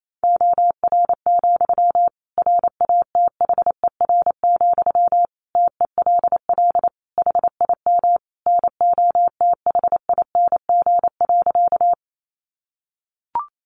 Morse code, heard as a keyed tone, about 28 wpm, 700 Hz. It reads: OR, RATHER, TELL HIM NOTHING.
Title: Sentences from Top 700 Words 28wpm